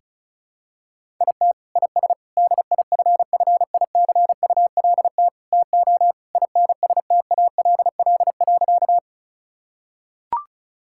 Morse code, heard as a keyed tone, about 36 wpm, 700 Hz. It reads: IT IS DIFFICULT TO INSTALL.
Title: Sentences from Top 1600 Words 35wpm